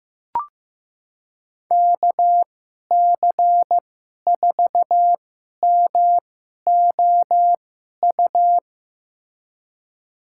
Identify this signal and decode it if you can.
Morse code, heard as a keyed tone, about 15 wpm, spaced as for 11 wpm, 700 Hz. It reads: KC4MOU
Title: USA Call Signs - 2x Character Spacing - 15wpm